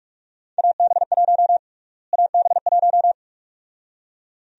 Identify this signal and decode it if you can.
Morse code, heard as a keyed tone, about 46 wpm, 700 Hz. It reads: AB1 AB1
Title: USA Call Sign Prefixes - Repeated 2x - 45wpm